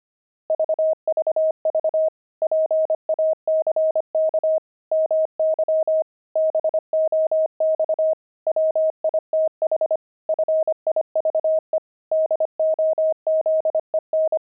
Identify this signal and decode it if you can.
Morse code, heard as a keyed tone, about 25 wpm, 630 Hz. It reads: VVV PACK MY BOX WITH FIVE DOZEN
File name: pangram-fs-25-cs-25-jugs